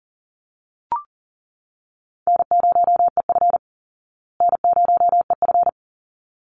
Morse code, heard as a keyed tone, about 40 wpm, 700 Hz. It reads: N0EF N0EF
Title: Easy Call Signs Repeated 2x - 40wpm